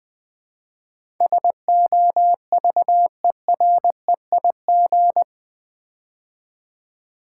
Morse code, heard as a keyed tone, about 20 wpm, 700 Hz. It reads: SOVEREIG
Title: Build A Word - 4 Letters At A Time - 11-Letter Words - 20wpm